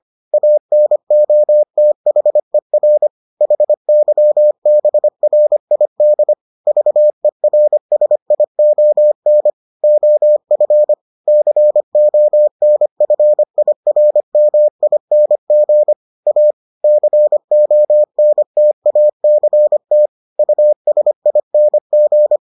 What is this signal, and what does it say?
Morse code, heard as a keyed tone, about 25 wpm, 600 Hz. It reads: ANOTHER HYBRID VERSION OF CONFIRMING A CONTACT USING